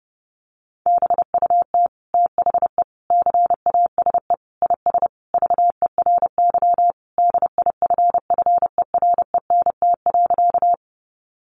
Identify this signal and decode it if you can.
Morse code, heard as a keyed tone, about 30 wpm, 700 Hz. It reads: BUT THE CASE IS VERY DIFFERENT.
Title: Sentences from Top 700 Words 30wpm